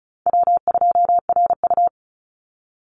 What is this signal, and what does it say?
Morse code, heard as a keyed tone, about 34 wpm, 700 Hz. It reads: W2RU
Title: Easy Call Signs Repeated 2x - 35wpm